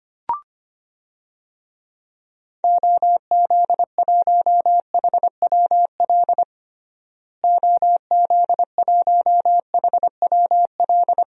Morse code, heard as a keyed tone, about 25 wpm, 700 Hz. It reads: OZ1HWL OZ1HWL
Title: DXCC-100 Call Signs - 6 Characters - Repeated 3x - 25wpm